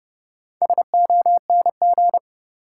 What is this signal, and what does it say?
Morse code, heard as a keyed tone, about 30 wpm, 700 Hz. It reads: SONG
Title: Sentences from Top 400 Words - New Word Drill - 30wpm